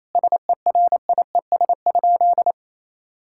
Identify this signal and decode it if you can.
Morse code, heard as a keyed tone, about 28 wpm, 700 Hz. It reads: SERIES?
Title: Sentences from Top 1000 Words - Review Part 5 - 28wpm